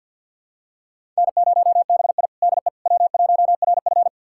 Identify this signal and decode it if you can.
Morse code, heard as a keyed tone, about 52 wpm, 700 Hz. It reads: N0BI DE W1RF
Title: USA Callsign DE Callsign - 4 Characters - 50wpm